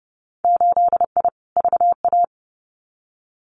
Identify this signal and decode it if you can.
Morse code, heard as a keyed tone, about 30 wpm, 700 Hz. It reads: 8I VA
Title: Winter Field Day Exchanges Encore 30wpm